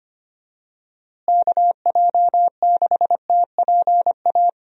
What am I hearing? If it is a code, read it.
Morse code, heard as a keyed tone, about 25 wpm, 700 Hz. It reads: KJ6TPA
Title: USA Call Signs - 6 Characters - Repeated 2x - 25wpm